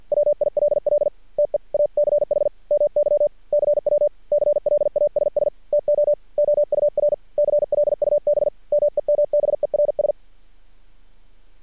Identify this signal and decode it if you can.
Morse code, heard as a keyed tone, about 50 wpm, 590 Hz. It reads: WILL TEACH MY CW CLASS TO OUR CLUB MEMBERS
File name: archived / CRSnet